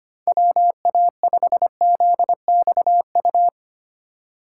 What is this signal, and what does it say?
Morse code, heard as a keyed tone, about 32 wpm, 700 Hz. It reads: WA5ZXU